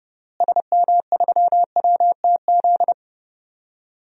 Morse code, heard as a keyed tone, about 30 wpm, 700 Hz. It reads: SM3WTZ
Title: DXCC-100 Call Signs - 6 Characters - Repeated 2x - 30wpm